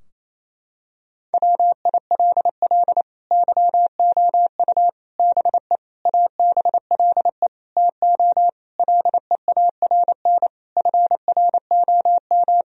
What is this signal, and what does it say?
Morse code, heard as a keyed tone, about 28 wpm, 700 Hz. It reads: WILL YOU BE ABLE TO LEARN FROM
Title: Sentences from Top 1000 Words - Review Part 2 - 28wpm